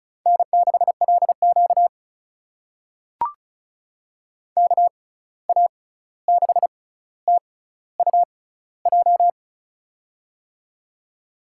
Morse code, heard as a keyed tone, about 36 wpm, spaced as for 16 wpm, 700 Hz. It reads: N6LQ KA6TUJ
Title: USA Call Signs - 6x Character Spacing - 35wpm